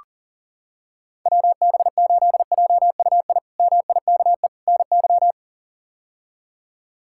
Morse code, heard as a keyed tone, about 40 wpm, 700 Hz. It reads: WB8JUI MIKE NY
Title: K1USN SST QSO Practice - 40wpm